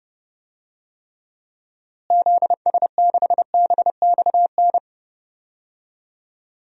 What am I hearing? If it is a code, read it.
Morse code, heard as a keyed tone, about 31 wpm, 700 Hz. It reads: ZS6BXN
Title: DXCC-100 Call Signs - 6 Characters - Repeated 3x - 30wpm